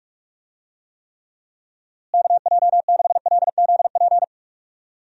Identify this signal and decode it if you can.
Morse code, heard as a keyed tone, about 42 wpm, 700 Hz. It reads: KJ6LZP
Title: USA Call Signs - 6 Characters - Repeated 2x - 45wpm